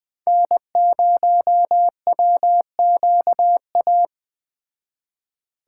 Morse code, heard as a keyed tone, about 20 wpm, 700 Hz. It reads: N0WQA
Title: Speed Racing - USA 1x3 Call Signs 17wpm